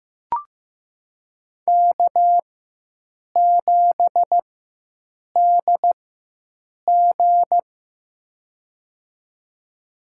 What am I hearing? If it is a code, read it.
Morse code, heard as a keyed tone, about 15 wpm, spaced as for 7 wpm, 700 Hz. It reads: K7DG
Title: USA Call Signs - 4 Characters - 4x Character Spacing - 15wpm